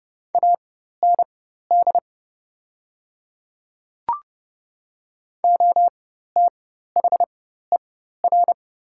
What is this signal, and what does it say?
Morse code, heard as a keyed tone, about 30 wpm, spaced as for 14 wpm, 700 Hz. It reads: AND OTHER
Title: 4x Character Spacing - Top 200 Words - 30wpm